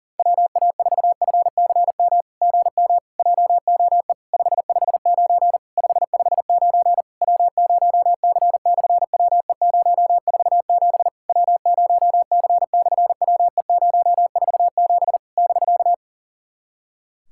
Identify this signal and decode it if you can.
Morse code, heard as a keyed tone, about 38 wpm, 690 Hz. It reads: WA4FCM GM JOE 559 559 W0C/WE047 W0C/WE047 <BK>
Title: SOTA QSO Practice 40wpm